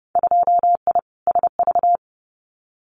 Morse code, heard as a keyed tone, about 31 wpm, 700 Hz. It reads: 2I SV
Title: Winter Field Day Exchanges Encore 30wpm